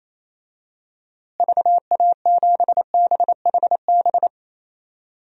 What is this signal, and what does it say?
Morse code, heard as a keyed tone, about 27 wpm, 700 Hz. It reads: VA7BHB
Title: DXCC-100 Call Signs - 6 Characters - Repeated 3x - 28wpm